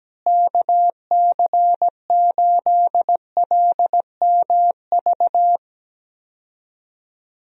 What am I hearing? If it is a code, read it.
Morse code, heard as a keyed tone, about 17 wpm, 700 Hz. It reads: KC8LMV
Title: USA Call Signs - 6 Characters - Repeated 2x - 17wpm